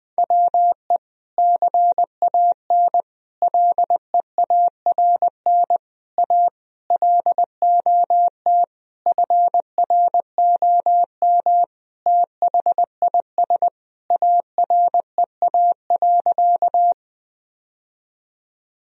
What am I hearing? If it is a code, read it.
Morse code, heard as a keyed tone, about 20 wpm, 700 Hz. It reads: WE CAN LEARN A LOT FROM THIS AREA.
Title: Sentences from Top 1000 Words - Review Part 2 - 20wpm